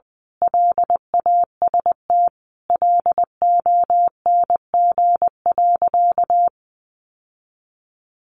Morse code, heard as a keyed tone, about 20 wpm, 700 Hz. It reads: LAST LONG.
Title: Sentences from Top 1300 Words 20wpm